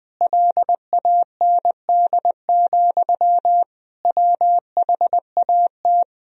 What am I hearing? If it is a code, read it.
Morse code, heard as a keyed tone, about 20 wpm, 700 Hz. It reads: LAND, WHAT
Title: Sentences from Top 1000 Words - Review Part 1 - 20wpm